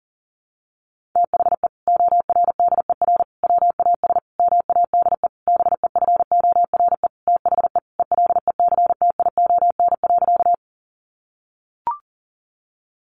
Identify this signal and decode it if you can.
Morse code, heard as a keyed tone, about 40 wpm, 700 Hz. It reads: THE ORDER WAS MADE BEFORE THE ELECTION.
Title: Sentences from Top 1500 Words 40wpm